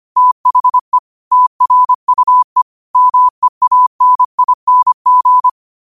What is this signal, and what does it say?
Morse code, heard as a keyed tone, about 25 wpm, 1000 Hz. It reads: THE TRUE MEANING
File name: qotdfiles / 25wpm